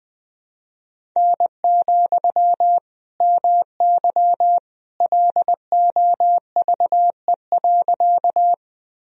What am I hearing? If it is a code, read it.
Morse code, heard as a keyed tone, about 20 wpm, 700 Hz. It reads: N, MY LOVE.
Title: Sentences from Top 400 Words and Abbreviations 20wpm